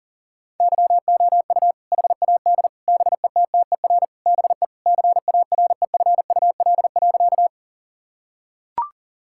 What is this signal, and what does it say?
Morse code, heard as a keyed tone, about 40 wpm, 700 Hz. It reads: YOU HAD BETTER BE CAREFUL.
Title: Sentences from Top 1600 Words 40wpm